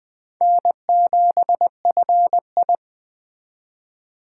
Morse code, heard as a keyed tone, about 21 wpm, 700 Hz. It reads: N7FI
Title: Speed Racing - USA Call Signs 17wpm